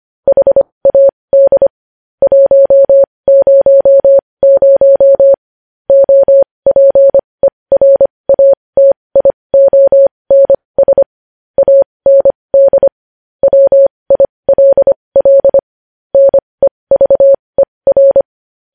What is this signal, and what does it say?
Morse code, heard as a keyed tone, about 25 wpm, spaced as for 20 wpm, 550 Hz. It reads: HAD 100 OPERATIONS AND WILL NEVER
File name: podcast / News headlines-1x-25x20